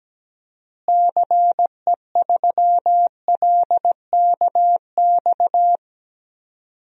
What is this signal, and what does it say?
Morse code, heard as a keyed tone, about 17 wpm, 700 Hz. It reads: CE3LKX
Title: DXCC-100 Call Signs - 6 Characters - Repeated 2x - 17wpm